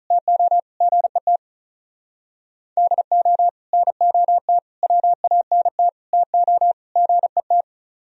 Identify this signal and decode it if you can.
Morse code, heard as a keyed tone, about 46 wpm, 700 Hz. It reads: TO GET DO NOT WANT TO GET